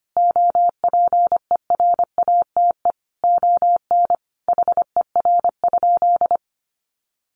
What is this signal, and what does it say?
Morse code, heard as a keyed tone, about 25 wpm, 700 Hz. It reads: OPERATE ON HER?
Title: Sentences from Top 1500 Words 25wpm